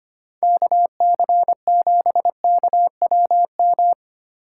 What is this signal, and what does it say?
Morse code, heard as a keyed tone, about 30 wpm, 700 Hz. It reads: KC7KWM
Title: Speed Racing - USA 2x3 Call Signs 25wpm